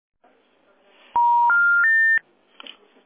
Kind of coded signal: telephone call progress tone